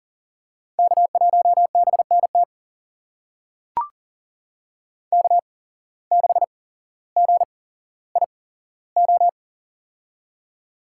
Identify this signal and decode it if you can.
Morse code, heard as a keyed tone, about 39 wpm, spaced as for 17 wpm, 700 Hz. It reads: K1BNT K6GIO
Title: USA Call Signs - 5 Characters - 8x Character Spacing - 40wpm